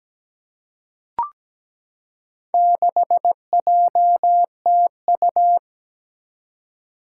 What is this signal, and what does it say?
Morse code, heard as a keyed tone, about 17 wpm, 700 Hz. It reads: 6JTU
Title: Four Challenging Letter-Number Combinations 17wpm